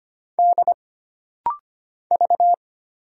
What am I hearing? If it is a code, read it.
Morse code, heard as a keyed tone, about 25 wpm, 700 Hz. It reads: D V
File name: Single Character (Letter:Number:Punctuation:Prosign) - Rapid-Fire - 25wpm